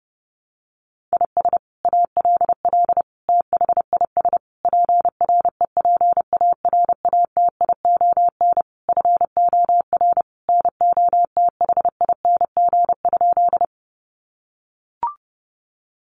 Morse code, heard as a keyed tone, about 30 wpm, 700 Hz. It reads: IS ALL THIS PREPARATION FOR NOTHING?
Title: Sentences from Top 1400 Words 30wpm